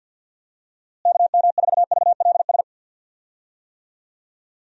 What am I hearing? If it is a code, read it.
Morse code, heard as a keyed tone, about 50 wpm, 700 Hz. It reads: KM4VLS
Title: USA Call Signs - 6 Characters - Repeated 3x - 50wpm